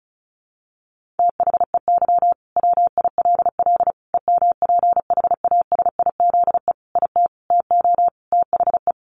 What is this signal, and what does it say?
Morse code, heard as a keyed tone, about 34 wpm, 700 Hz. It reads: THEY WILL EMPHASIZE IT TO THE
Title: Sentences from Top 1500 Words 35wpm